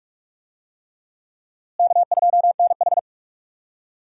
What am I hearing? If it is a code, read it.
Morse code, heard as a keyed tone, about 45 wpm, 700 Hz. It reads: K2NH
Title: USA Call Signs - 4 Characters - Repeated 3x - 45wpm